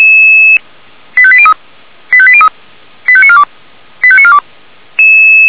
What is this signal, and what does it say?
Five-tone selective call (ZVEI), heard as quick runs of five tones, tones of about 70 ms. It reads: 85803 85803 85942 85942